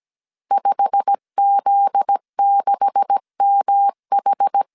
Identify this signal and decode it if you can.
Morse code, heard as a keyed tone, about 17 wpm, 800 Hz. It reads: 5Z6MH
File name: CW / DET / 1 9 1500C S C 16wpm 2909